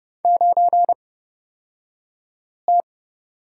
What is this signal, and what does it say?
Morse code, heard as a keyed tone, about 30 wpm, spaced as for 5 wpm, 690 Hz. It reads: 9T
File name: Lesson 36d - BT - Two Characters 30wpm